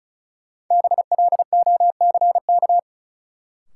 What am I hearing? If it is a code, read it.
Morse code, heard as a keyed tone, about 36 wpm, 700 Hz. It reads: BLOCK
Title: Sentences from Top 1600 Words - New Word Drill Alt - 35wpm